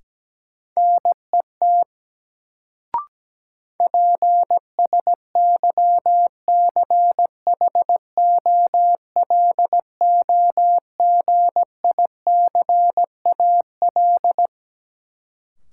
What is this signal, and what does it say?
Morse code, heard as a keyed tone, about 17 wpm, 700 Hz. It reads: NET PSYCHOLOGICAL
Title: Sentences from Top 1400 Words - New Word Drill Alt - 17wpm